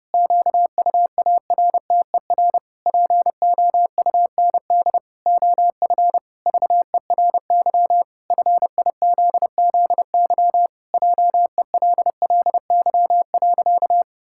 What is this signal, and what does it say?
Morse code, heard as a keyed tone, about 30 wpm, 700 Hz. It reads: QUARTER POUND OF VERY FIZZY JELLY.
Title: Pangram Challenge v2 - 30wpm